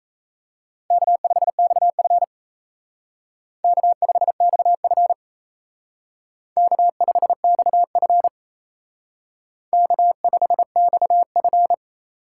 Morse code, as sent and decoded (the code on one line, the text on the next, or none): -.- ..... -..- ..-. / -.- ..... -..- ..-. / -.- ..... -..- ..-. / -.- ..... -..- ..-.
K5XF K5XF K5XF K5XF